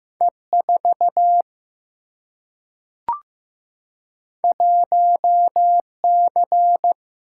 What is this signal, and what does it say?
Morse code, heard as a keyed tone, about 15 wpm, 700 Hz. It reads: E 4 1 C
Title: Single-Letter Single-Number 15wpm